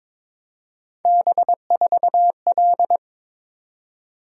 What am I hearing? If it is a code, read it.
Morse code, heard as a keyed tone, about 22 wpm, 700 Hz. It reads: B4L
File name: practice / 22wpm / Three Letter-Number Combinations - 22wpm